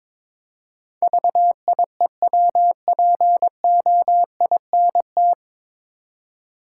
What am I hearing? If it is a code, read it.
Morse code, heard as a keyed tone, about 22 wpm, 700 Hz. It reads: VIEWPOINT
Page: Morse Code Ninja practice 22wpm